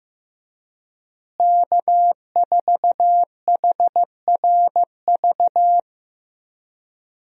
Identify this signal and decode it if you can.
Morse code, heard as a keyed tone, about 15 wpm, 700 Hz. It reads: K4HRV
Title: USA Call Signs - 5 Characters - Repeated 2x - 15wpm